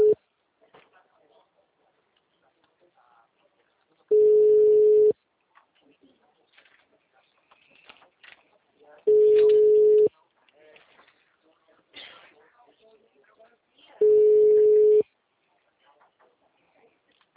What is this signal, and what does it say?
Ringback tone (Europe, 425 Hz), with speech.
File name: out- -2002- - - .25918